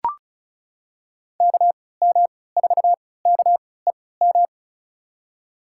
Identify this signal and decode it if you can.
Morse code, heard as a keyed tone, about 34 wpm, spaced as for 20 wpm, 700 Hz. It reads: KM4KEM
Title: USA Call Signs - 6 Characters - 3x Character Spacing - 35wpm